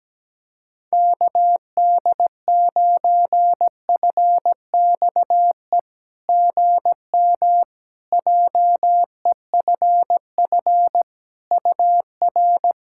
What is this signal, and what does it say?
Morse code, heard as a keyed tone, about 17 wpm, 700 Hz. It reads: KD9FXE GM JEFF UR
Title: SOTA QSO Practice 17wpm